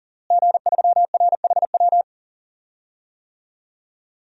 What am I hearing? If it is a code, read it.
Morse code, heard as a keyed tone, about 40 wpm, 700 Hz. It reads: G3RHW
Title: DXCC-100 Call Signs - 5 Characters - Repeated 3x - 40wpm